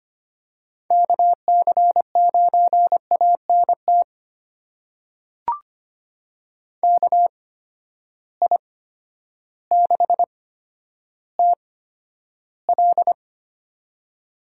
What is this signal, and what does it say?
Morse code, heard as a keyed tone, about 25 wpm, spaced as for 11 wpm, 700 Hz. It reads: KC9ANT KI6TL
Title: USA Call Signs - 8x Character Spacing - 25wpm